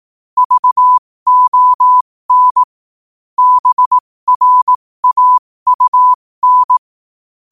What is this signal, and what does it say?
Morse code, heard as a keyed tone, about 18 wpm, spaced as for 15 wpm, 1000 Hz. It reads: VON BRAUN
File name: qotdfiles / 15wpm